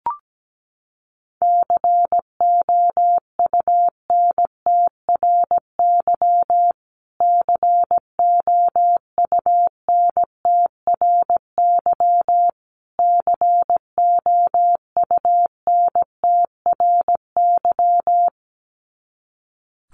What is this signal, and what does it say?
Morse code, heard as a keyed tone, about 17 wpm, 700 Hz. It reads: COUNTRY COUNTRY COUNTRY
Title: Top 500 Words Repeated 17wpm